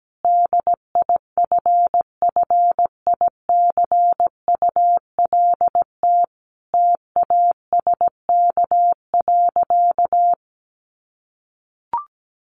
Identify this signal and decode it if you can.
Morse code, heard as a keyed tone, about 17 wpm, 700 Hz. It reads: DIFFICULT TASK.
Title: Sentences from Top 1300 Words 17wpm